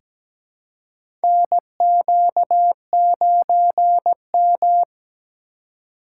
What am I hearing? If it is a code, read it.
Morse code, heard as a keyed tone, about 17 wpm, 700 Hz. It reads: NQ9M
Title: USA Call Signs - 4 Characters - 2x Character Spacing - 17wpm